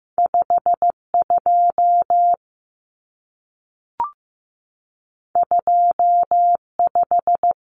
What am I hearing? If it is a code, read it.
Morse code, heard as a keyed tone, about 15 wpm, 700 Hz. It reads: 5 2 2 5
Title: Numbers - 2-Digit 15wpm